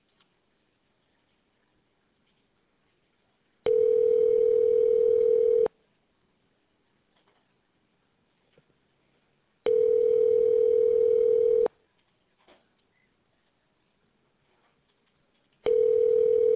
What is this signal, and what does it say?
Ringback tone (North America), 440 + 480 Hz.